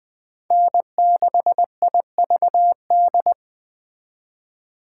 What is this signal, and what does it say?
Morse code, heard as a keyed tone, about 20 wpm, 700 Hz. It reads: N6IVD